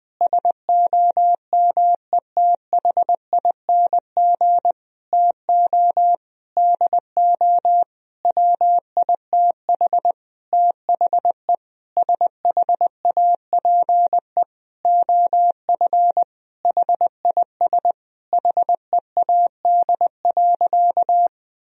Morse code, heard as a keyed tone, about 20 wpm, 700 Hz. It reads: SOMETHING TO DO WITH THE SHAPE OF HIS HEAD.
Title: Sentences from Top 1000 Words - Review Part 3 - 20wpm